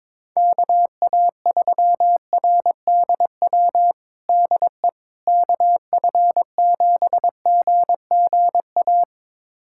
Morse code, heard as a keyed tone, about 22 wpm, 700 Hz. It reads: KA3RDW DE KF7GGA
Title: USA Callsign DE Callsign 22wpm